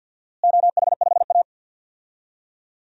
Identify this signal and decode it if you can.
Morse code, heard as a keyed tone, about 51 wpm, 700 Hz. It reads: OH5A
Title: DXCC-100 Call Signs - 4 Characters - Repeated 3x - 50wpm